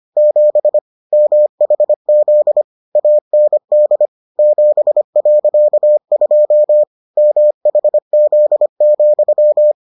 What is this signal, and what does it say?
Morse code, heard as a keyed tone, about 25 wpm, 600 Hz. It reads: 7 MHZ AND 7.2 MHZ,